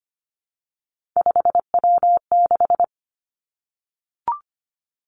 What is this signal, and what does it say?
Morse code, heard as a keyed tone, about 25 wpm, 700 Hz. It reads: 5W6
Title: Three Challenging Letter-Number Combinations 25wpm